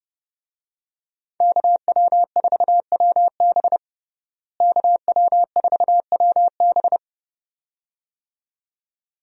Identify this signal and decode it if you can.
Morse code, heard as a keyed tone, about 30 wpm, 700 Hz. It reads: KW4WB KW4WB
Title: USA Call Signs - 5 Characters - Repeated 2x - 30wpm